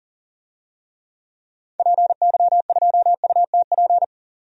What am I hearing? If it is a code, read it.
Morse code, heard as a keyed tone, about 40 wpm, 700 Hz. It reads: PY2UTP
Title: DXCC-100 Call Signs - 6 Characters - 40wpm